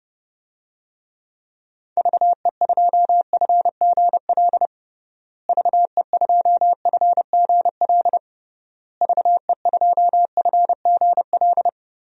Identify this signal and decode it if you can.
Morse code, heard as a keyed tone, about 30 wpm, 700 Hz. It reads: VE2FGL VE2FGL VE2FGL
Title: DXCC-100 Call Signs - 6 Characters - Repeated 3x - 30wpm